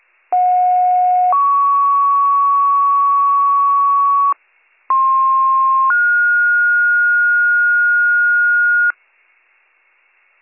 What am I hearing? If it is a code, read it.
Two-tone page, heard as a one-second tone, then a three-second tone, about 730 then 1120 Hz, about 1060 then 1430 Hz. It is heard from 0.3 to 8.9 s.